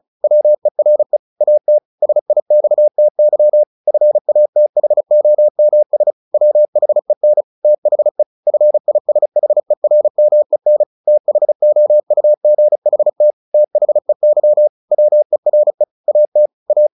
Morse code, heard as a keyed tone, about 34 wpm, 600 Hz. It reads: WERE AT SIXTY FATHOMS WHEN THE FISHERMEN THOUGHT THEY WERE AT A
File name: the old man and the sea / the old man and the sea 1 35wpm